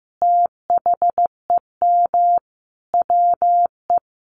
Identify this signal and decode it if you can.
Morse code, heard as a keyed tone, about 15 wpm, 700 Hz. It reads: THEM WE
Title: Sentences from Top 1300 Words 15wpm